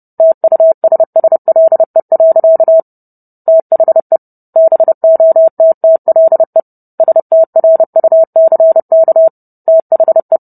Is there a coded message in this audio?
Yes: Morse code from 0.2 to 10.4 s, heard as a keyed tone, about 30 wpm, 650 Hz. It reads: TUSSLE. THE BOTTLE STRUCK THE